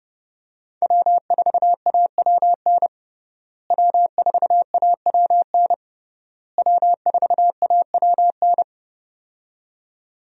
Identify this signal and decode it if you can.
Morse code, heard as a keyed tone, about 30 wpm, 700 Hz. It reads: W4AWN W4AWN W4AWN
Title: USA Call Signs - 5 Characters - Repeated 3x - 30wpm